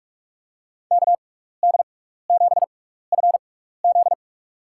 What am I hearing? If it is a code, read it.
Morse code, heard as a keyed tone, about 44 wpm, spaced as for 15 wpm, 700 Hz. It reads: KD7FZ
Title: USA Call Signs - 5 Characters - 6x Character Spacing - 45wpm